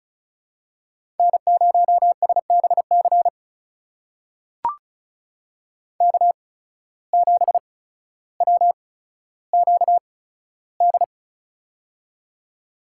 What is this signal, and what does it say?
Morse code, heard as a keyed tone, about 36 wpm, spaced as for 15 wpm, 700 Hz. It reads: N0SBC K7WQD
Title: USA Call Signs - 8x Character Spacing - 35wpm